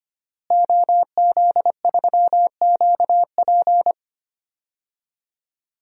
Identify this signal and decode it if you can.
Morse code, heard as a keyed tone, about 25 wpm, 700 Hz. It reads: OZ3QP
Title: DXCC-100 Call Signs - 5 Characters - Repeated 3x - 25wpm